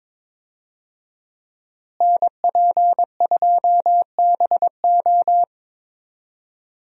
Morse code, heard as a keyed tone, about 22 wpm, 700 Hz. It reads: NP2BO
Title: USA Call Signs - 5 Characters - Repeated 2x - 22wpm